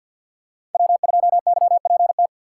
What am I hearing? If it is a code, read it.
Morse code, heard as a keyed tone, about 46 wpm, 700 Hz. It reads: W2YPT
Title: USA Call Signs - 5 Characters - 3x Character Spacing - 50wpm